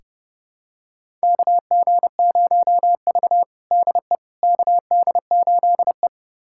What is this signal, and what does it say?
Morse code, heard as a keyed tone, about 32 wpm, 700 Hz. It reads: KG0V DE KD8E
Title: USA Callsign DE Callsign - 4 Characters - 30wpm